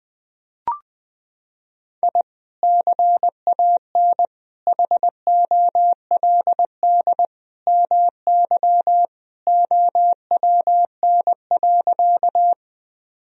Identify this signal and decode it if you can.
Morse code, heard as a keyed tone, about 20 wpm, 700 Hz. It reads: I CAN HOLD MY OWN.
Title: Sentences from Top 1000 Words - Review Part 2 - 20wpm